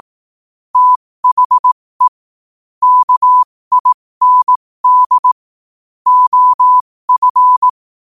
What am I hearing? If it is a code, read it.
Morse code, heard as a keyed tone, about 18 wpm, spaced as for 15 wpm, 1000 Hz. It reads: THE KIND OF
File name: qotdfiles / 15wpm